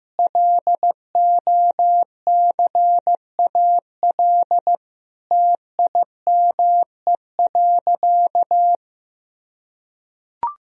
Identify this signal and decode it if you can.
Morse code, heard as a keyed tone, about 16 wpm, 700 Hz. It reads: LOCAL TIME.
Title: Sentences from Top 1000 Words - Review Part 3 - 15wpm